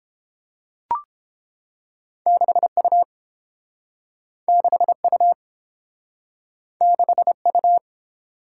Morse code, heard as a keyed tone, about 30 wpm, 700 Hz. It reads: 6U 6U 6U